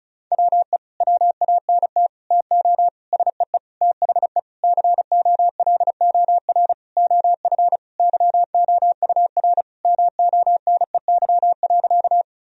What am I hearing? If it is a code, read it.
Morse code, heard as a keyed tone, about 36 wpm, 700 Hz. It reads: WE WANT TO SEE THE COLOR OF YOUR MONEY.
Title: Sentences from Top 1000 Words - Review Part 2 - 35wpm